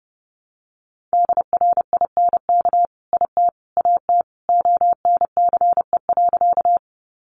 Morse code, heard as a keyed tone, about 30 wpm, 700 Hz. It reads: DRINK IT AT ONCE.
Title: Sentences from Top 1500 Words 30wpm